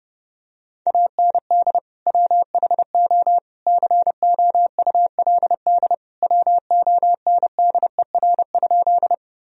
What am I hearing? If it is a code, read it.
Morse code, heard as a keyed tone, about 32 wpm, 700 Hz. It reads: AND WHO COULD WONDER?
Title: Sentences from Top 1000 Words - Review Part 3 - 30wpm